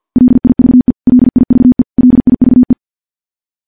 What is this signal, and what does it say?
Morse code, heard as a keyed tone, about 50 wpm, 250 Hz. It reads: LIVE LIVE LIVE
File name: morse / morse-training-50WPM